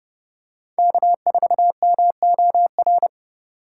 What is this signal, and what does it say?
Morse code, heard as a keyed tone, about 29 wpm, 700 Hz. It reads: K4MOR